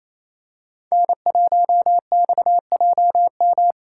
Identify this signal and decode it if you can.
Morse code, heard as a keyed tone, about 28 wpm, 700 Hz. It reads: N1XJM
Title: K1USN SST QSO Practice - 28wpm